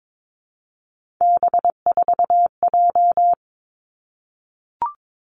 Morse code, heard as a keyed tone, about 22 wpm, 700 Hz. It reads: B4J
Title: Three Challenging Letter-Number Combinations 22wpm